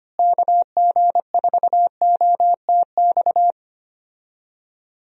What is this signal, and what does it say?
Morse code, heard as a keyed tone, about 25 wpm, 700 Hz. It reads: KG4OTX
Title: USA Call Signs - 6 Characters - Repeated 3x - 25wpm